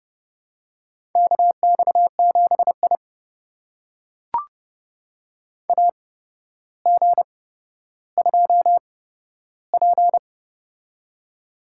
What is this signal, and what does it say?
Morse code, heard as a keyed tone, about 30 wpm, spaced as for 13 wpm, 700 Hz. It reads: KX7I AG2P
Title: USA Call Signs - 4 Characters - 8x Character Spacing - 30wpm